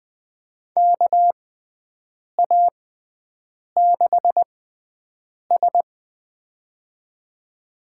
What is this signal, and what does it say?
Morse code, heard as a keyed tone, about 20 wpm, spaced as for 7 wpm, 700 Hz. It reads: KA6S
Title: USA Call Signs - 4 Characters - 6x Character Spacing - 20wpm